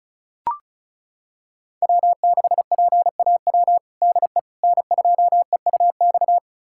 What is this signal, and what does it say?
Morse code, heard as a keyed tone, about 34 wpm, 700 Hz. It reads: W6PAW DE N2EUX
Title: USA Callsign DE Callsign - 5 Characters - 35wpm